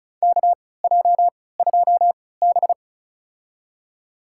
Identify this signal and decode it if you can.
Morse code, heard as a keyed tone, about 36 wpm, spaced as for 20 wpm, 700 Hz. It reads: KJ2B
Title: USA Call Signs - 4 Characters - 3x Character Spacing - 35wpm